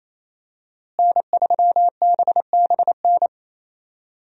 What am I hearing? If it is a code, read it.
Morse code, heard as a keyed tone, about 28 wpm, 700 Hz. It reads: N3BBN
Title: USA Call Signs - 5 Characters - Repeated 2x - 28wpm